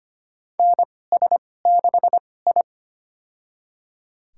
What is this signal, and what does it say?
Morse code, heard as a keyed tone, about 25 wpm, spaced as for 18 wpm, 700 Hz. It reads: NS6I
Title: USA Call Signs - 4 Characters - 2x Character Spacing - 25wpm